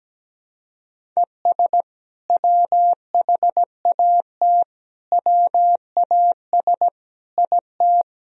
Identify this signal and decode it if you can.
Morse code, heard as a keyed tone, about 17 wpm, 700 Hz. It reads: ES WHAT WAS IT
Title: Sentences from Top 100 Words and Abbreviations 17wpm